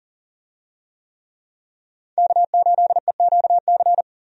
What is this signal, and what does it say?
Morse code, heard as a keyed tone, about 40 wpm, 700 Hz. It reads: K8EQC
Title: USA Call Signs - 5 Characters - 40wpm